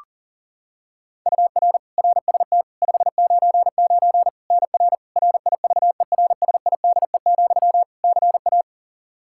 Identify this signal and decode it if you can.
Morse code, heard as a keyed tone, about 37 wpm, 700 Hz. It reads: UR RST 599 NR RIVERSIDE, CA
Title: Practice QSO - Exchange 1 - Beginning - 40wpm